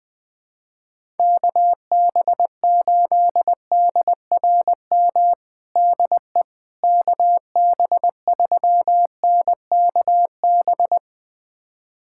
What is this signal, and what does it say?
Morse code, heard as a keyed tone, about 20 wpm, 700 Hz. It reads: KB8DRM DE KB3NKB
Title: USA Callsign DE Callsign - 6 Characters - 20wpm